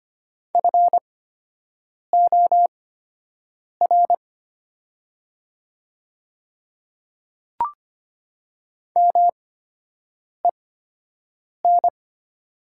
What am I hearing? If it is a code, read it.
Morse code, heard as a keyed tone, about 25 wpm, spaced as for 7 wpm, 700 Hz. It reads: FOR MEN
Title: 8x Character Spacing - Top 200 Words - 25wpm